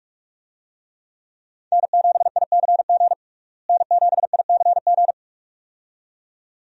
Morse code, heard as a keyed tone, about 46 wpm, 700 Hz. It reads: N7ICG N7ICG
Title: USA Call Signs - 5 Characters - Repeated 2x - 45wpm